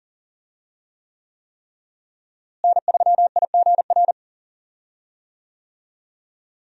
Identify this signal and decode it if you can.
Morse code, heard as a keyed tone, about 40 wpm, 700 Hz. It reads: N3IGR
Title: USA Call Signs - 5 Characters - Repeated 3x - 40wpm